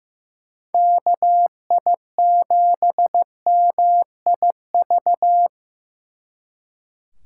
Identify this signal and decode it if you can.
Morse code, heard as a keyed tone, about 18 wpm, 700 Hz. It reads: KI7MIV
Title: Speed Racing - USA 2x3 Call Signs 15wpm